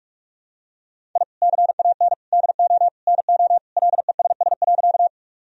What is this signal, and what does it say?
Morse code, heard as a keyed tone, about 43 wpm, 700 Hz. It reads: I CAN DO NO LESS.
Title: Sentences from Top 1000 Words - Review Part 2 - 45wpm